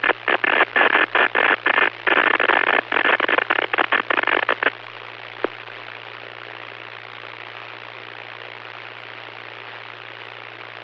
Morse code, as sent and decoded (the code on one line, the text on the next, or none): none